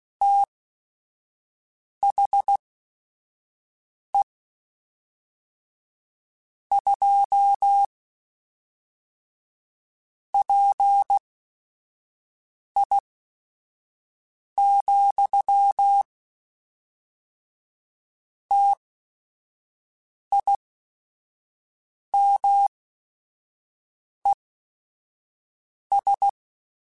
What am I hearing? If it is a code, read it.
Morse code, heard as a keyed tone, about 16 wpm, spaced as for 5 wpm, 800 Hz. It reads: THE 2 PI, TIMES
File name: Technical Subjects / Technical Subjects-mp3s / XL @5